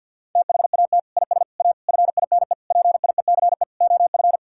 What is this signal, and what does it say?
Morse code, heard as a keyed tone, about 52 wpm, 700 Hz. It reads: THAT IS A FINE PIECE OF
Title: Sentences from Top 1000 Words - Review Part 5 - 50wpm